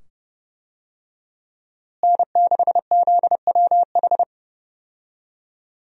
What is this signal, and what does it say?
Morse code, heard as a keyed tone, about 30 wpm, 700 Hz. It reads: N6ZWH
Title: USA Call Signs - 5 Characters - Repeated 2x - 30wpm